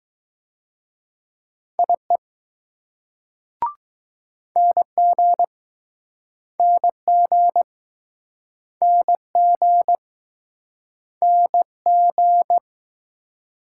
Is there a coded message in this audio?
Yes: Morse code from 1.8 to 12.6 s, heard as a keyed tone, about 20 wpm, 700 Hz. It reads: IE NG NG NG NG